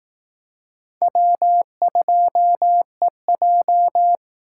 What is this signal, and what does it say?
Morse code, heard as a keyed tone, about 18 wpm, 700 Hz. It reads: W2EJ
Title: Speed Racing - USA 1x2 Call Signs 15wpm